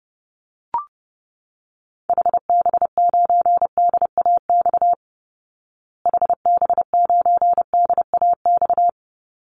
Morse code, heard as a keyed tone, about 30 wpm, 700 Hz. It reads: HB9DAX HB9DAX
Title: Call Signs Repeated 2x - 30wpm